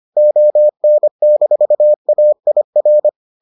Morse code, heard as a keyed tone, about 25 wpm, 600 Hz. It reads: ON-AIR